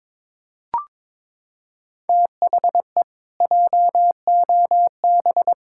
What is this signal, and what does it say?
Morse code, heard as a keyed tone, about 22 wpm, 700 Hz. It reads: THE JOB
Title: Pangram Challenge v2 - 22wpm